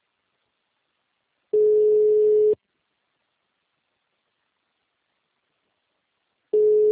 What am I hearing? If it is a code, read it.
Ringback tone (Europe, 425 Hz).